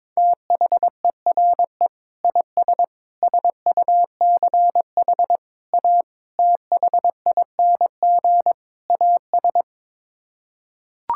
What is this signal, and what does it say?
Morse code, heard as a keyed tone, about 22 wpm, 700 Hz. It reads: THERE IS SUCH A THING AS
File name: practice / 22wpm / Sets of 6 Words - 22wpm